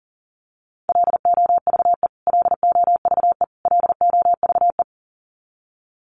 Morse code, heard as a keyed tone, about 40 wpm, 700 Hz. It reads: LOVE LOVE LOVE
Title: Top 500 Words Repeated 40wpm